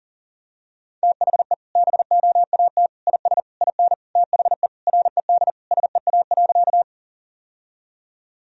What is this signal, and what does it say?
Morse code, heard as a keyed tone, about 40 wpm, 700 Hz. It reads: THE BOAT IS IN THE RED SEA.
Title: Sentences from Top 1000 Words - Review Part 2 - 40wpm